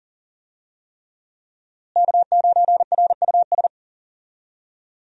Morse code, heard as a keyed tone, about 39 wpm, 700 Hz. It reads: K9RUS
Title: USA Call Signs - 5 Characters - Repeated 3x - 40wpm